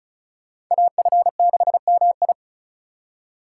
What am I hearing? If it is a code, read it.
Morse code, heard as a keyed tone, about 44 wpm, 700 Hz. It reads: AF6MI